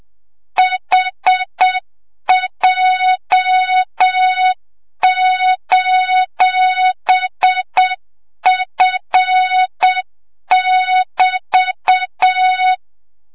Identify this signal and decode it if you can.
Morse code, heard as a keyed tone, about 7 wpm, 3700 Hz, with noise. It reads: HJ:F<BT>
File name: Morze-HA2MN / hang / 25 GYAKORLAT